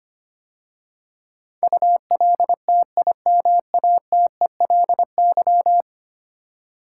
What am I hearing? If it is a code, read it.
Morse code, heard as a keyed tone, about 25 wpm, 700 Hz. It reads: ULTIMATELY
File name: 10 Letter Words - First 3 Letters Given - 25wpm